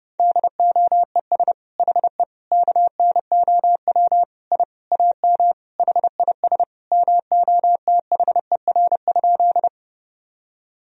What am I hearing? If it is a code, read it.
Morse code, heard as a keyed tone, about 30 wpm, 700 Hz. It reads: DOES HE KNOW I AM HIS MOTHER?
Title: Sentences from Top 1000 Words - Review Part 2 - 30wpm